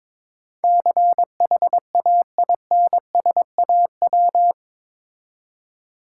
Morse code, heard as a keyed tone, about 21 wpm, 690 Hz. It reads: CHAINSAW
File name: Difficult Words with Hints - 22wpm